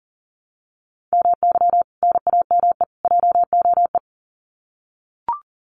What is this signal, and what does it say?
Morse code, heard as a keyed tone, about 40 wpm, 700 Hz. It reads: MY NAME JOE
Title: Common QSO Phrases 40wpm